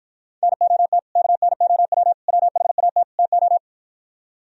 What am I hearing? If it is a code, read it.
Morse code, heard as a keyed tone, about 53 wpm, 700 Hz. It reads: NOT KNOW WHAT TO